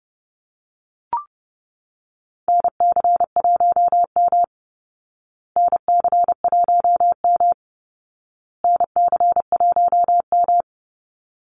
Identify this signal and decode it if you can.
Morse code, heard as a keyed tone, about 30 wpm, 700 Hz. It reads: NC1M NC1M NC1M
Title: Call Signs Repeated 2x - 30wpm